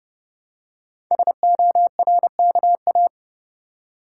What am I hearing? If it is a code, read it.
Morse code, heard as a keyed tone, about 28 wpm, 700 Hz. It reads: SORKA
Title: Code Groups - 5 Letters - 30wpm